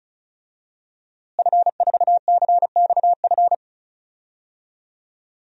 Morse code, heard as a keyed tone, about 34 wpm, 700 Hz. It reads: F4CXF
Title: DXCC-100 Call Signs - 5 Characters - Repeated 3x - 35wpm